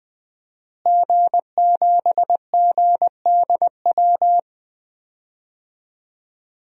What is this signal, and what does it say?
Morse code, heard as a keyed tone, about 20 wpm, 700 Hz. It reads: G7GDW
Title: DXCC-100 Call Signs - 5 Characters - Repeated 2x - 20wpm